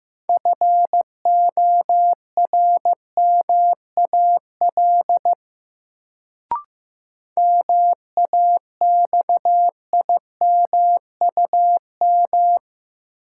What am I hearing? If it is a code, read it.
Morse code, heard as a keyed tone, about 15 wpm, 700 Hz. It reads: FORMAL MAXIMUM
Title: Sentences from Top 1300 Words - New Word Drill - 15wpm